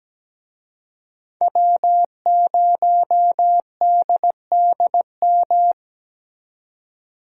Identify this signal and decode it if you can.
Morse code, heard as a keyed tone, about 20 wpm, 700 Hz. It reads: W0DDM
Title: Speed Racing - USA 1x3 Call Signs 17wpm